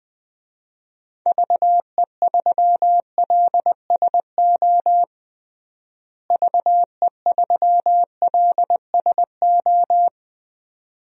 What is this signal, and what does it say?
Morse code, heard as a keyed tone, about 20 wpm, 700 Hz. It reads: VE3LSO VE3LSO
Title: DXCC-100 Call Signs - 6 Characters - Repeated 2x - 20wpm